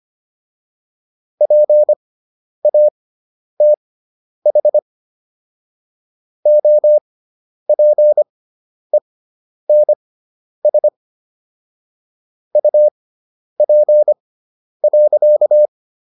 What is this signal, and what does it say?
Morse code, heard as a keyed tone, about 25 wpm, spaced as for 10 wpm, 600 Hz. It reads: PATH OPENS UP.